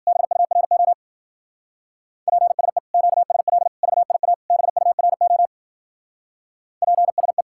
Morse code, heard as a keyed tone, about 46 wpm, 700 Hz. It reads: BURO PSE QSL VIA BURO PSE
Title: Speed-Racing Common QSO Phrases - 40wpm